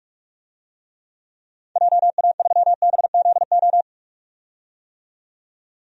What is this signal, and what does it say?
Morse code, heard as a keyed tone, about 42 wpm, 700 Hz. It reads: JA3BZO
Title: DXCC-100 Call Signs - 6 Characters - 45wpm